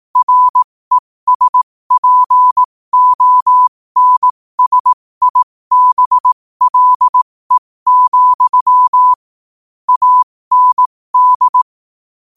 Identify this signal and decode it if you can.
Morse code, heard as a keyed tone, about 18 wpm, spaced as for 15 wpm, 1000 Hz. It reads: RESPONSIBLE, AND
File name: qotdfiles / 15wpm